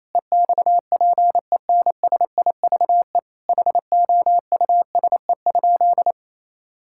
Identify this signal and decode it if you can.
Morse code, heard as a keyed tone, about 27 wpm, 700 Hz. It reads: EXPENSIVE HOUSE?
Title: Sentences from Top 1000 Words - Review Part 6 - 28wpm